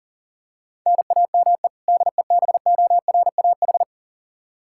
Morse code, heard as a keyed tone, about 37 wpm, 700 Hz. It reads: NAME DEBORAH
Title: Practice QSO - Exchange 1 - Beginning - 40wpm